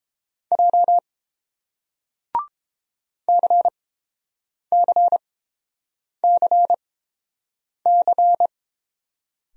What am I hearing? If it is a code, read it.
Morse code, heard as a keyed tone, about 31 wpm, spaced as for 7 wpm, 700 Hz. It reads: J CCCC